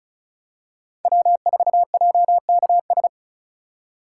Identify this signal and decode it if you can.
Morse code, heard as a keyed tone, about 36 wpm, 700 Hz. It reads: W4JKS
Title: USA Call Signs - 5 Characters - 2x Character Spacing - 35wpm